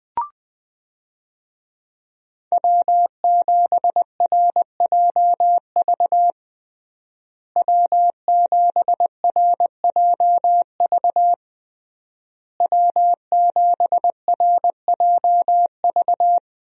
Morse code, heard as a keyed tone, about 20 wpm, 700 Hz. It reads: W7RJV W7RJV W7RJV
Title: USA Call Signs - 5 Characters - Repeated 3x - 20wpm